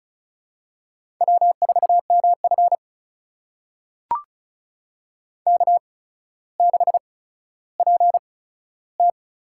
Morse code, heard as a keyed tone, about 36 wpm, spaced as for 15 wpm, 700 Hz. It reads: W4MF K6PT
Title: USA Call Signs - 4 Characters - 8x Character Spacing - 35wpm